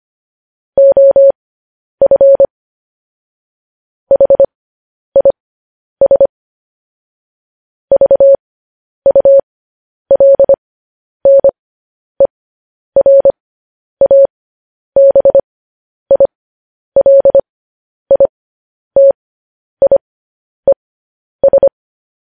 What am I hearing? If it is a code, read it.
Morse code, heard as a keyed tone, about 25 wpm, spaced as for 10 wpm, 550 Hz. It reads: OF HIS VULNERABILITIES
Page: news-headlines-in-morse-code-at-10-wpm